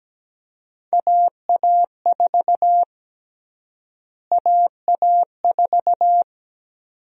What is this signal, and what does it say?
Morse code, heard as a keyed tone, about 17 wpm, 700 Hz. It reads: AA4 AA4
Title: USA Call Sign Prefixes - Repeated 2x - 17wpm